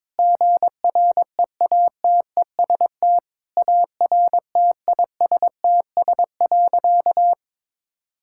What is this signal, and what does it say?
Morse code, heard as a keyed tone, about 22 wpm, 700 Hz. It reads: GREATEST ARTISTS.
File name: practice / 22wpm / Sentences with 8-Word Phrases - 22wpm